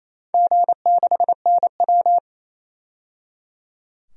Morse code, heard as a keyed tone, about 28 wpm, 700 Hz. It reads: G6NW
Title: DXCC-100 Call Signs - 4 Characters - 28wpm